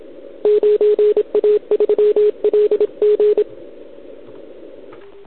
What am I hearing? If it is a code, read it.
Morse code, heard as a keyed tone, about 30 wpm, 420 Hz, with radio noise. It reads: 9A3LG
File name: ARCHIV / OKDX2019 / OKOM40 KX3